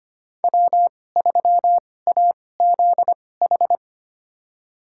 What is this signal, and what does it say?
Morse code, heard as a keyed tone, about 25 wpm, spaced as for 18 wpm, 700 Hz. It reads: W3AZH
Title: USA Call Signs - 2x Character Spacing - 25wpm